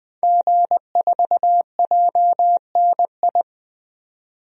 Morse code, heard as a keyed tone, about 20 wpm, 700 Hz. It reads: G4JNI
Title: DXCC-100 Call Signs - 5 Characters - 20wpm